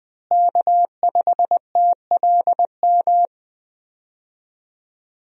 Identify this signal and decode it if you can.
Morse code, heard as a keyed tone, about 20 wpm, 700 Hz. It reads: K5TLM
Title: USA Call Signs - 5 Characters - Repeated 3x - 20wpm